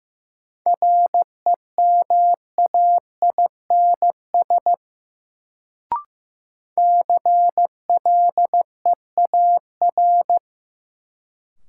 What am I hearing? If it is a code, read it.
Morse code, heard as a keyed tone, about 15 wpm, 700 Hz. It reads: REMAINS CLEAR
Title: Sets of 5 Words Encore - New Word Drill - 15wpm